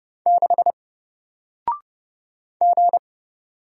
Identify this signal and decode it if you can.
Morse code, heard as a keyed tone, about 30 wpm, 700 Hz. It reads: 6 G
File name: Single-Letter-Number-30wpm